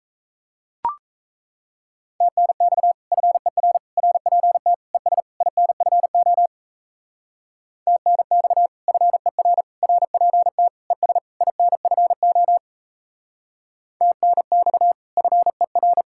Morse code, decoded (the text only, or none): TNX FER RPT ES INFO TNX FER RPT ES INFO TNX FER